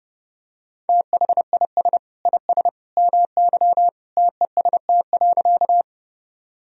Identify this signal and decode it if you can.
Morse code, heard as a keyed tone, about 32 wpm, 700 Hz. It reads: THIS IS MY TEST.
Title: Sentences from Top 1000 Words - Review Part 3 - 30wpm